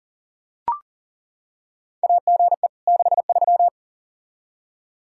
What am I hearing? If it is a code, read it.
Morse code, heard as a keyed tone, about 40 wpm, 700 Hz. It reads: AGE 63
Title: Practice QSO - Exchange 3 - More Info - 40wpm